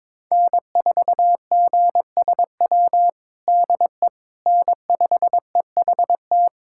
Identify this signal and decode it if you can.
Morse code, heard as a keyed tone, about 22 wpm, 700 Hz. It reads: N4GSW DE N5EHT
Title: USA Callsign DE Callsign - 5 Characters - 22wpm